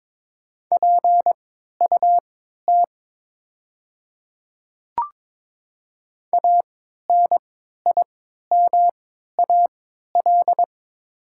Morse code, heard as a keyed tone, about 22 wpm, spaced as for 13 wpm, 690 Hz. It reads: PUT ANIMAL
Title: 3x Character Spacing - Top 200 Words - 22wpm